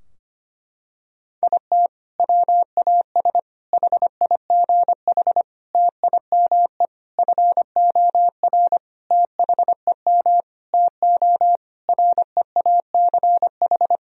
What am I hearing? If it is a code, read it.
Morse code, heard as a keyed tone, about 25 wpm, 700 Hz. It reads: IT WAS HIGH TIME FOR THEM TO REACH
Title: Sentences from Top 1000 Words - Review Part 2 - 25wpm